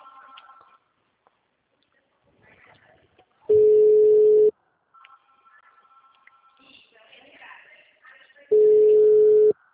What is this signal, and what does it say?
Ringback tone (Europe, 425 Hz).